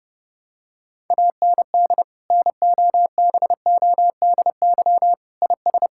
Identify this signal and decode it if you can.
Morse code, heard as a keyed tone, about 29 wpm, 700 Hz. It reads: AND NOBODY IS
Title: Sentences from Top 1600 Words 30wpm